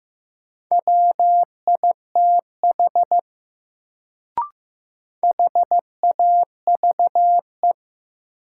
Morse code, heard as a keyed tone, about 15 wpm, 700 Hz. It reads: WITH HAVE
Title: Sentences from Top 100 Words - New Word Drill - 15wpm